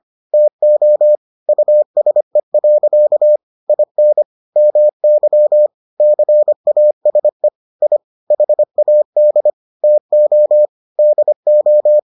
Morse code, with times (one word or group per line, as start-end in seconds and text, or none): 0.3-1.2 TO
1.5-3.4 USE.
3.7-4.2 IN
4.6-5.7 MY
6.0-7.5 CASE
7.8-8.0 I
8.3-9.5 HAD
9.8-10.7 TO
11.0-12.0 DO